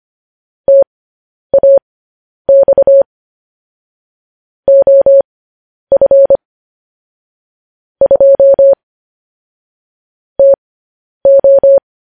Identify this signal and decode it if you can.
Morse code, heard as a keyed tone, about 25 wpm, spaced as for 10 wpm, 550 Hz. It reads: TAX OF 2 TO
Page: News Headlines in Morse Code at 10 WPM